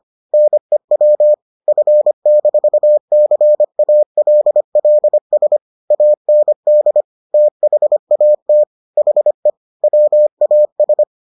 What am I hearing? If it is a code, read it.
Morse code, heard as a keyed tone, about 25 wpm, 600 Hz. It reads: NEW F-CALLS AND THAT HE WAS